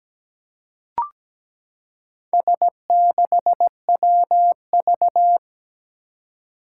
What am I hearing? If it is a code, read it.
Morse code, heard as a keyed tone, about 17 wpm, 700 Hz. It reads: S6WV
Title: Four Challenging Letter-Number Combinations 17wpm